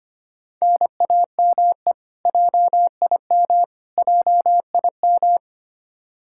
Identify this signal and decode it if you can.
Morse code, heard as a keyed tone, about 25 wpm, 700 Hz. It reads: NAME JIM JIM
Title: Practice QSO - Exchange 1 - Beginning - 25wpm